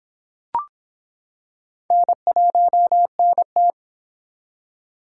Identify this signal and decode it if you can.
Morse code, heard as a keyed tone, about 25 wpm, 700 Hz. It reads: N1NT